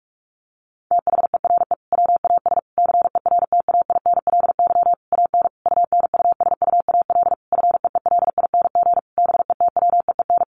Morse code, heard as a keyed tone, about 44 wpm, 700 Hz. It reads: THERE WAS CERTAINLY AN UNUSUAL FEELING BETWEEN
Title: Sentences from Top 1400 Words 45wpm